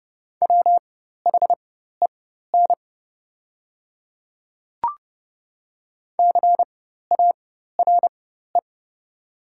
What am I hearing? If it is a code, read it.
Morse code, heard as a keyed tone, about 30 wpm, spaced as for 14 wpm, 700 Hz. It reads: WHEN CARE
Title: 4x Character Spacing - Top 300 Words - 30wpm